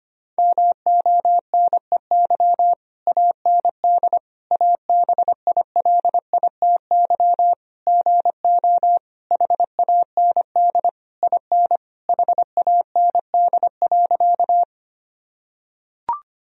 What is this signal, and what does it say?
Morse code, heard as a keyed tone, about 25 wpm, 700 Hz. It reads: MONEY AND ABILITY GO HAND IN HAND.
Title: Sentences from Top 1000 Words - Review Part 3 - 25wpm